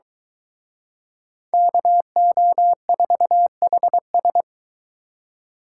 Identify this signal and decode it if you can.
Morse code, heard as a keyed tone, about 23 wpm, 700 Hz. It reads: KO4HS